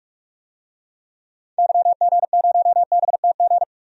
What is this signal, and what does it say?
Morse code, heard as a keyed tone, about 46 wpm, 700 Hz. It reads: YG0BTG
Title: DXCC-100 Call Signs - 6 Characters - 45wpm